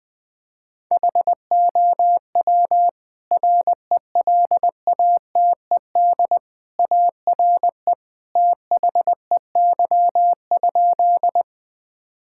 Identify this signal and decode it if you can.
Morse code, heard as a keyed tone, about 21 wpm, 700 Hz. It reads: HOW RELATED ARE THEY?
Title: Sentences from Top 1000 Words - Review Part 5 - 20wpm